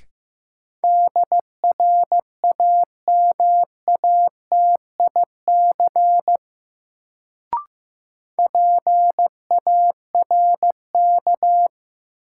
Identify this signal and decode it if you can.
Morse code, heard as a keyed tone, about 16 wpm, 700 Hz. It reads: DRAMATIC PARK
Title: Sentences from Top 1300 Words - New Word Drill Alt - 15wpm